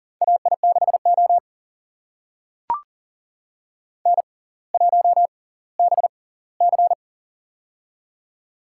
Morse code, heard as a keyed tone, about 38 wpm, spaced as for 20 wpm, 690 Hz. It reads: AI6O N1BC
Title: USA Call Signs - 4 Characters - 6x Character Spacing - 40wpm